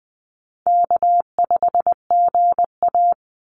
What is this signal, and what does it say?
Morse code, heard as a keyed tone, about 20 wpm, 700 Hz. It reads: K5GA
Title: Easy Call Signs Repeated 3x - 20wpm